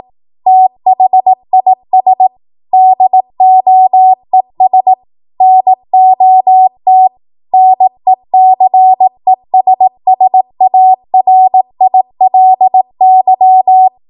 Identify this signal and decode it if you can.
Morse code, heard as a keyed tone, about 18 wpm, 750 Hz. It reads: THIS DOES NOT NECESSARILY
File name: Morse / 18WPM